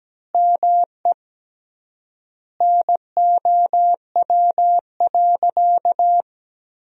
Morse code, heard as a keyed tone, about 17 wpm, 700 Hz. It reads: ME NOW.
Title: 3x Word-Spacing - Sentences from Top 500 Words - 17wpm